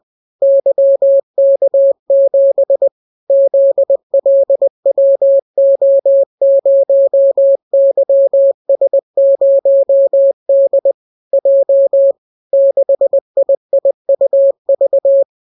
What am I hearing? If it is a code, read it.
Morse code, heard as a keyed tone, about 20 wpm, 550 Hz. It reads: YK7 ZLWO0YS0D J 6IIUV